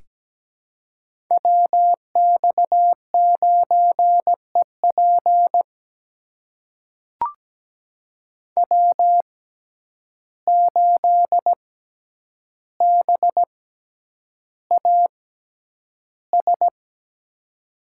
Morse code, heard as a keyed tone, about 17 wpm, spaced as for 9 wpm, 700 Hz. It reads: WX9EP W8BAS
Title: USA Call Signs - 5 Characters - 6x Character Spacing - 17wpm